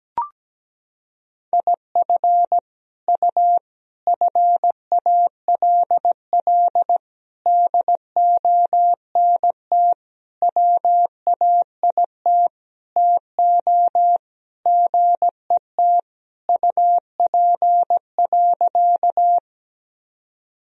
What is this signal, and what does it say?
Morse code, heard as a keyed tone, about 17 wpm, 700 Hz. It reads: IF U FALL DONT WAIT TO GET UP.
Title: Sentences from Top 500 Words and Abbreviations 17wpm